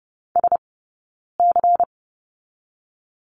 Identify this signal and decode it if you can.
Morse code, heard as a keyed tone, about 30 wpm, spaced as for 9 wpm, 700 Hz. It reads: SC
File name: Lesson 20d - Number 3 - Two Characters 30wpm